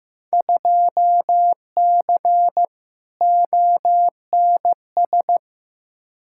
Morse code, heard as a keyed tone, about 15 wpm, 700 Hz. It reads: 2C ONS
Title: Field Day Exchanges Encore - 15wpm